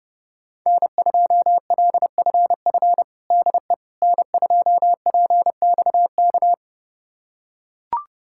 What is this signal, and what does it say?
Morse code, heard as a keyed tone, about 32 wpm, 700 Hz. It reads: N2LFF DE N2PXK
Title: USA Callsign DE Callsign - 5 Characters - 30wpm